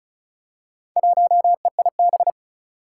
Morse code, heard as a keyed tone, about 36 wpm, 700 Hz. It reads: 1EIB
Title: Four Challenging Letter-Number Combinations 35wpm